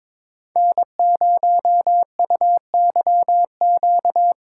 Morse code, heard as a keyed tone, about 23 wpm, 700 Hz. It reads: N0UYQ
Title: USA Call Signs - 5 Characters - 8x Character Spacing - 22wpm